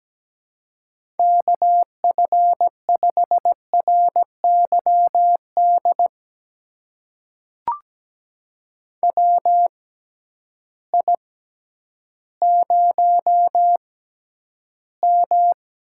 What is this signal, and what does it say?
Morse code, heard as a keyed tone, about 17 wpm, spaced as for 10 wpm, 700 Hz. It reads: KF5RYD WI0M
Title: USA Call Signs - 6x Character Spacing - 17wpm